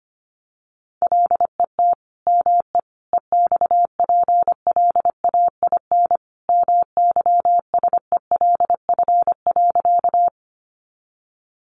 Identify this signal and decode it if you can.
Morse code, heard as a keyed tone, about 25 wpm, 700 Hz. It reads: LET ME EXPLAIN MYSELF.
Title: Sentences from Top 1300 Words 25wpm